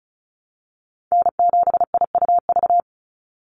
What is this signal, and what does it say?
Morse code, heard as a keyed tone, about 34 wpm, 700 Hz. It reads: N7IUV
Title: USA Call Signs 35wpm